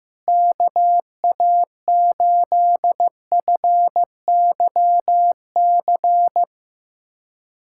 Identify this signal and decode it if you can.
Morse code, heard as a keyed tone, about 15 wpm, 700 Hz. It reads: KA8FYC
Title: USA Callsign DE Callsign - 6 Characters - 15wpm